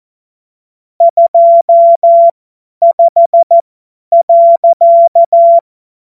Morse code, heard as a keyed tone, about 14 wpm, spaced as for 10 wpm, 680 Hz. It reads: 25.
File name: Morsekurs / Witze / Witz 06. Juli 2021, 50 F70